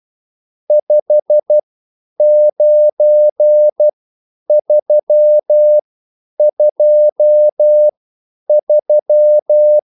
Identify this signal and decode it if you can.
Morse code, heard as a keyed tone, about 12 wpm, spaced as for 9 wpm, 600 Hz. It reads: 59323